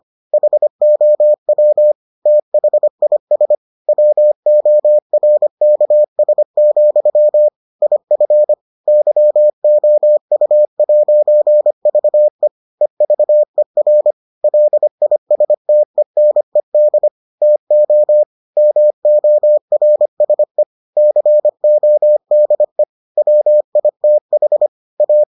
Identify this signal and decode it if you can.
Morse code, heard as a keyed tone, about 25 wpm, 600 Hz. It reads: HOW THIS WORKS, IF YOU'VE EVER LISTENED TO MORSE CODE WITH A